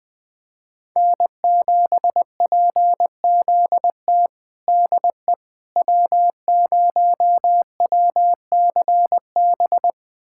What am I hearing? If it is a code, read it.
Morse code, heard as a keyed tone, about 20 wpm, 700 Hz. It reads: N7PZT DE W0WCB
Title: USA Callsign DE Callsign 20wpm